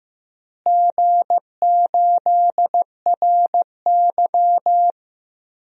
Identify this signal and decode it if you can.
Morse code, heard as a keyed tone, about 15 wpm, 700 Hz. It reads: G8RY
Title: DXCC-100 Call Signs - 4 Characters - Repeated 3x - 15wpm